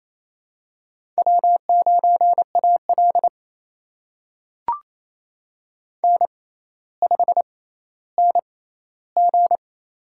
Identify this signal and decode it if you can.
Morse code, heard as a keyed tone, about 28 wpm, spaced as for 14 wpm, 700 Hz. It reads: W9AL N5NG
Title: USA Call Signs - 4 Characters - 6x Character Spacing - 28wpm